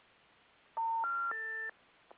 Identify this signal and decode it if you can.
Special information tone.
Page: S-series, P-series, R-series, and M-series recordings were recorded using a pickup coil. No digital amplification was performed, to preserve the original levels.